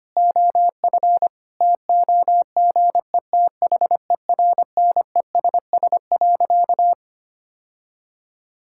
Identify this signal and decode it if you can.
Morse code, heard as a keyed tone, about 25 wpm, 700 Hz. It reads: OF TOGETHERNESS.
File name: practice / 25wpm / Difficult Sentences with Hints - 25wpm